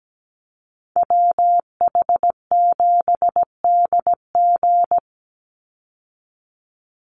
Morse code, heard as a keyed tone, about 17 wpm, 700 Hz. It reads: WH7DG
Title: USA Call Signs 17wpm